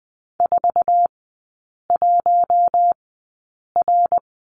Morse code, heard as a keyed tone, about 20 wpm, spaced as for 8 wpm, 700 Hz. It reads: 41R
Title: 074 Lesson 13e - Letter L - Three Characters 20wpm